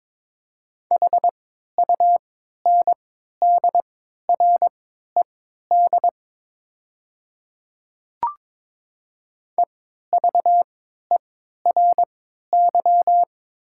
Morse code, heard as a keyed tone, about 22 wpm, spaced as for 13 wpm, 700 Hz. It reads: HUNDRED EVERY
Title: 3x Character Spacing - Top 500 Words - 22wpm